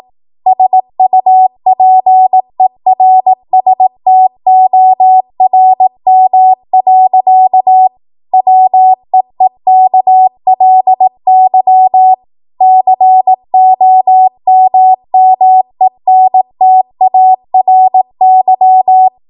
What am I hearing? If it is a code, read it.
Morse code, heard as a keyed tone, about 18 wpm, 750 Hz. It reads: SUPERSTORM. WEEKLY COMMENTARY